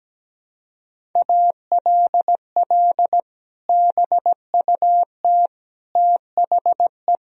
Morse code, heard as a keyed tone, about 17 wpm, 700 Hz. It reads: ALL BUT THE
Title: Sentences from Top 1000 Words - Review Part 1 - 17wpm